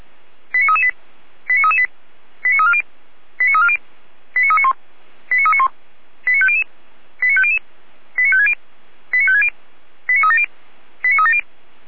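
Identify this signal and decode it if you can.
Five-tone selective call (ZVEI), heard as quick runs of five tones, tones of about 80 ms. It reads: 89338 89338 89340 89340 89391 89391 89500 89500 89560 89560 89370 89370
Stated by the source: ZVEI five-tone call received by SDR.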